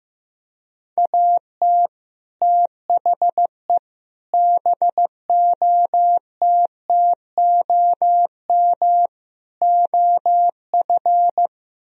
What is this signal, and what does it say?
Morse code, heard as a keyed tone, about 15 wpm, 700 Hz. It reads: AT THE BOTTOM OF
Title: Speed-Racing - Sets of 4 Words 15wpm